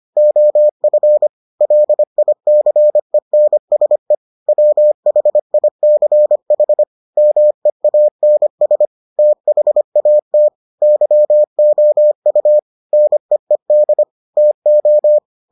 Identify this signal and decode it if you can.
Morse code, heard as a keyed tone, about 25 wpm, 600 Hz. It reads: OF LICENSE WHICH MEANS THAT YOU NEED TO